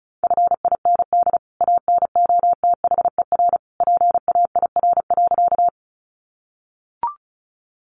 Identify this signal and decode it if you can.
Morse code, heard as a keyed tone, about 34 wpm, 700 Hz. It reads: FIND ANOTHER PAIR.
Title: Sentences from Top 1200 Words 35wpm